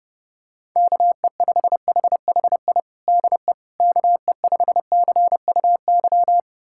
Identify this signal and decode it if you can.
Morse code, heard as a keyed tone, about 30 wpm, 700 Hz. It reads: KE5HHI DE KE5CUY
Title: USA Callsign DE Callsign - 6 Characters - 30wpm